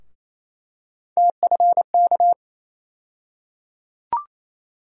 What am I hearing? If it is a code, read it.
Morse code, heard as a keyed tone, about 27 wpm, 700 Hz. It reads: TFK
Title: USA Call Sign Postfixes - 3 Letters 28wpm